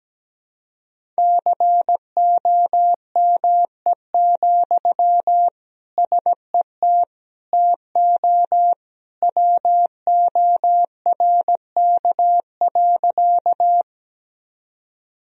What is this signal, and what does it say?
Morse code, heard as a keyed tone, about 17 wpm, 700 Hz. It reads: COME, SET TO WORK.
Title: Sentences from Top 1000 Words - Review Part 1 - 17wpm